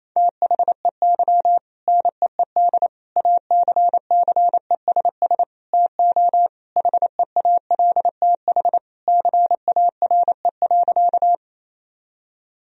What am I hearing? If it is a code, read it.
Morse code, heard as a keyed tone, about 28 wpm, 700 Hz. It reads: THEY NEED ACCESS TO HEALTH CARE.
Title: Sentences from Top 1000 Words - Review Part 4 - 28wpm